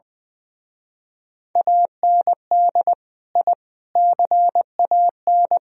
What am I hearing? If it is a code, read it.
Morse code, heard as a keyed tone, about 24 wpm, 700 Hz. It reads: AND I CAN
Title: Speed-Racing - Sets of 4 Words 20wpm